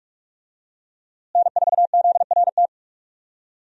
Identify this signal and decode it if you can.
Morse code, heard as a keyed tone, about 41 wpm, 700 Hz. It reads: N4ZRT
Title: USA Call Signs - 5 Characters - Repeated 3x - 45wpm